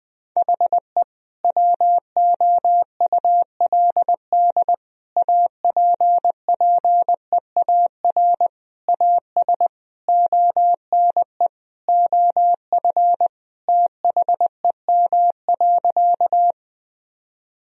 Morse code, heard as a keyed tone, about 21 wpm, 700 Hz. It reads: HE WOULD APPEAR AS ONE OF THEM.
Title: Sentences from Top 1000 Words - Review Part 2 - 20wpm